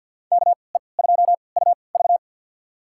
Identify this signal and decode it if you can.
Morse code, heard as a keyed tone, about 52 wpm, spaced as for 29 wpm, 700 Hz. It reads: KE2UV
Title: USA Call Signs - 5 Characters - 3x Character Spacing - 50wpm